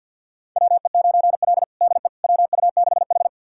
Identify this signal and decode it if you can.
Morse code, heard as a keyed tone, about 52 wpm, 700 Hz. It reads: WE9L DE WU6H
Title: USA Callsign DE Callsign - 4 Characters - 50wpm